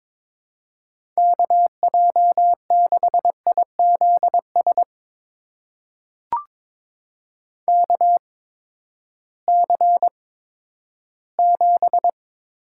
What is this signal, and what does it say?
Morse code, heard as a keyed tone, about 22 wpm, spaced as for 12 wpm, 700 Hz. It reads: KJ6IZS KC7
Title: USA Call Signs - 8x Character Spacing - 22wpm